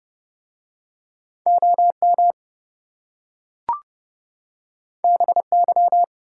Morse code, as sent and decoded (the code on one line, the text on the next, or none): --- -- / -... -.--
OM BY